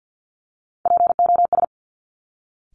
Morse code, heard as a keyed tone, about 50 wpm, 700 Hz. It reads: POS
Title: Top Three Letter Combinations in Words 50wpm